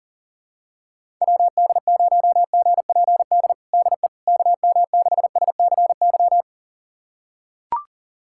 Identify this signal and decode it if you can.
Morse code, heard as a keyed tone, about 40 wpm, 700 Hz. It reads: WD0GPD DE KM6SCY
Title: USA Callsign DE Callsign 40wpm